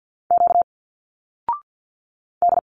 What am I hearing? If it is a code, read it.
Morse code, heard as a keyed tone, about 51 wpm, 700 Hz. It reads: Q D
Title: Single Letters 50wpm